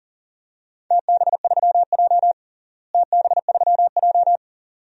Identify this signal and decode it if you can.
Morse code, heard as a keyed tone, about 40 wpm, 700 Hz. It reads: TB3J TB3J
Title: DXCC-100 Call Signs - 4 Characters - Repeated 2x - 40wpm